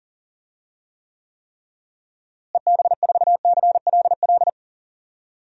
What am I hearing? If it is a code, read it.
Morse code, heard as a keyed tone, about 40 wpm, 700 Hz. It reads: EB4CLL
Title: DXCC-100 Call Signs - 6 Characters - Repeated 3x - 40wpm